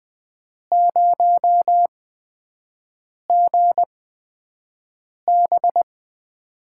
Morse code, heard as a keyed tone, about 20 wpm, spaced as for 5 wpm, 700 Hz. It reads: 0GB
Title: USA Call Signs - 4 Characters - 8x Character Spacing - 20wpm